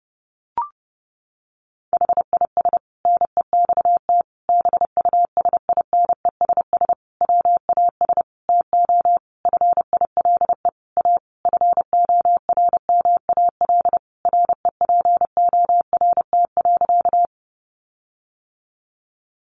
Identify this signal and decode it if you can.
Morse code, heard as a keyed tone, about 30 wpm, 700 Hz. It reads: HIS NEXT BUSINESS WAS TO FILE A FORMAL REPORT.
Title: Sentences from Top 1300 Words 30wpm